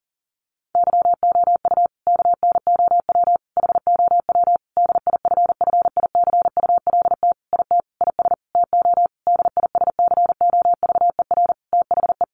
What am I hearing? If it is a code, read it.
Morse code, heard as a keyed tone, about 40 wpm, 700 Hz. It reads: YOU KNOW HOW DIFFICULT IT IS TO DISCOVER THE
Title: Sentences from Top 1100 Words 40wpm